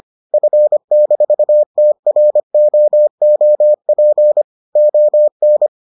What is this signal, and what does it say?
Morse code, heard as a keyed tone, about 25 wpm, 600 Hz. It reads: F-TROOP ON